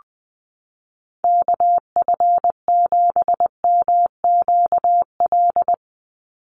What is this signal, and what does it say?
Morse code, heard as a keyed tone, about 20 wpm, 700 Hz. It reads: KF7MQL
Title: USA Call Signs 20wpm